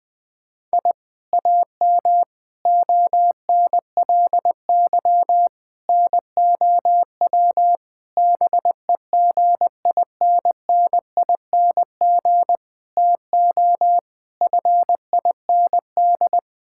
Morse code, heard as a keyed tone, about 21 wpm, 700 Hz. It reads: I AM ONLY NOW BEGINNING TO FIND
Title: Sentences from Top 1000 Words - Review Part 5 - 20wpm